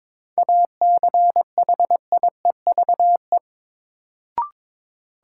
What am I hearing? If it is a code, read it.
Morse code, heard as a keyed tone, about 22 wpm, 700 Hz. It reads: ACHIEVE
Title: Sentences from Top 1000 Words - Review Part 5 - New Word Drill - 22wpm